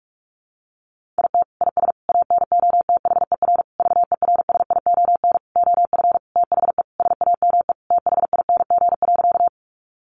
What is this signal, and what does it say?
Morse code, heard as a keyed tone, about 44 wpm, 700 Hz. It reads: IT IS ANOTHER VERSION OF THE SAME THING.
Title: Sentences from Top 1400 Words 45wpm